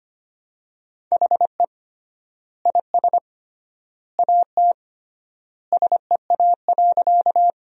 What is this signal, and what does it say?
Morse code, heard as a keyed tone, about 25 wpm, 700 Hz. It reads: HE IS AT SEA.
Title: 3x Word-Spacing - Sentences from Top 300 Words - 25wpm